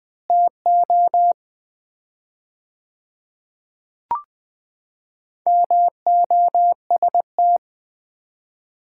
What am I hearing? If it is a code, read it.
Morse code, heard as a keyed tone, about 20 wpm, 700 Hz. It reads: TO MOST
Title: 4x Word-Spacing - Sets of 4 Words - 20wpm